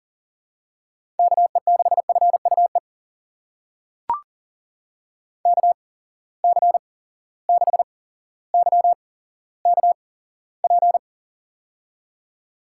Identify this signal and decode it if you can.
Morse code, heard as a keyed tone, about 38 wpm, spaced as for 17 wpm, 700 Hz. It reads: KE6FUE KC6YKP
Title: USA Call Signs - 6 Characters - 8x Character Spacing - 40wpm